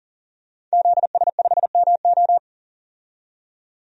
Morse code, heard as a keyed tone, about 40 wpm, 700 Hz. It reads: ZS5MO
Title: DXCC-100 Call Signs - 5 Characters - 40wpm